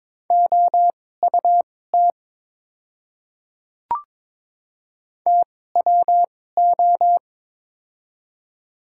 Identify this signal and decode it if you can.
Morse code, heard as a keyed tone, about 22 wpm, spaced as for 16 wpm, 700 Hz. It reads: OUT TWO
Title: 2x Character Spacing - Top 100 Words - 22wpm